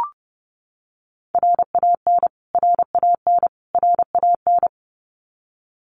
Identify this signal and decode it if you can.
Morse code, heard as a keyed tone, about 30 wpm, 700 Hz. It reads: RAN RAN RAN
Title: Top 500 Words Repeated 30wpm